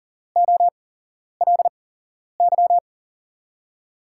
Morse code, heard as a keyed tone, about 42 wpm, spaced as for 11 wpm, 700 Hz. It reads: OLY
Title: USA Call Signs - 5 Characters - 8x Character Spacing - 40wpm